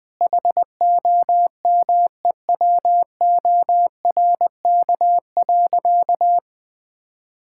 Morse code, heard as a keyed tone, about 21 wpm, 700 Hz. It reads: HOMEWORK.
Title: Sentences with 3-Word Phrases - Spoken Beginning - 20wpm